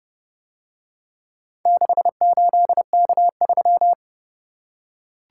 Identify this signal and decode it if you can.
Morse code, heard as a keyed tone, about 29 wpm, 700 Hz. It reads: 68K3